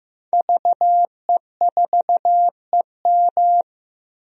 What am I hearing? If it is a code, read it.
Morse code, heard as a keyed tone, about 15 wpm, 700 Hz. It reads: VE4EM
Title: DXCC-100 Call Signs - 5 Characters - 15wpm